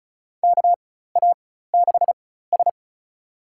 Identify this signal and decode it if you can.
Morse code, heard as a keyed tone, about 36 wpm, spaced as for 16 wpm, 700 Hz. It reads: KA6S
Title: USA Call Signs - 4 Characters - 4x Character Spacing - 35wpm